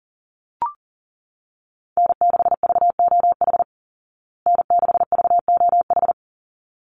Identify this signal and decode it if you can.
Morse code, heard as a keyed tone, about 40 wpm, 700 Hz. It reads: N6VOH N6VOH
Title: Call Signs Repeated 3x - 40wpm